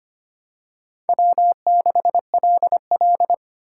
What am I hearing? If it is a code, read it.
Morse code, heard as a keyed tone, about 25 wpm, 700 Hz. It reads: W6LL
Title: USA Call Signs - 4 Characters - 3x Character Spacing - 25wpm